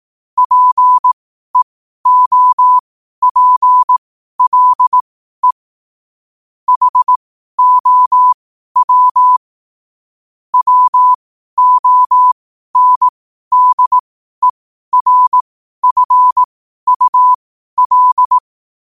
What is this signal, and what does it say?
Morse code, heard as a keyed tone, about 18 wpm, spaced as for 12 wpm, 1000 Hz. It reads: PEOPLE HOW WONDERFUL
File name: qotdfiles / 12wpm